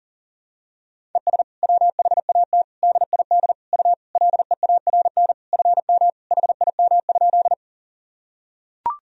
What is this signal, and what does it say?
Morse code, heard as a keyed tone, about 40 wpm, 700 Hz. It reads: ES WHAT DID U LEARN FM HIM?
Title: Sentences from Top 300 Words and Abbreviations 40wpm